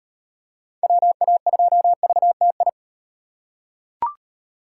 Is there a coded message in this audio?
Yes: Morse code from 0.8 to 2.7 s, heard as a keyed tone, about 38 wpm, 700 Hz. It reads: WA2VTI